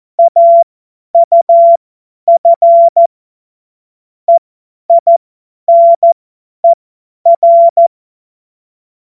Morse code, heard as a keyed tone, about 14 wpm, spaced as for 10 wpm, 680 Hz. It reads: AUF EINER
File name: Morsekurs / Witze / Witz 10. Mai 2022 50 F70